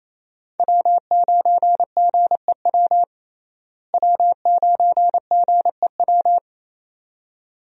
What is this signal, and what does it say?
Morse code, heard as a keyed tone, about 28 wpm, 700 Hz. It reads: W9GEW W9GEW
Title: USA Call Signs - 5 Characters - Repeated 2x - 28wpm